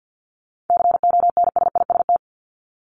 Morse code, heard as a keyed tone, about 50 wpm, 700 Hz. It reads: CONSIST
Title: Top 1251-1500 Words 50wpm